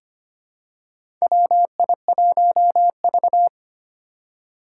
Morse code, heard as a keyed tone, about 25 wpm, 700 Hz. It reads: WI1V
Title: USA Call Signs - 4 Characters - 2x Character Spacing - 25wpm